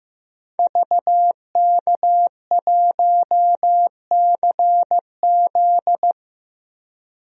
Morse code, heard as a keyed tone, about 16 wpm, 700 Hz. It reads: VK1CZ
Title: DXCC-100 Call Signs - 5 Characters - Repeated 3x - 15wpm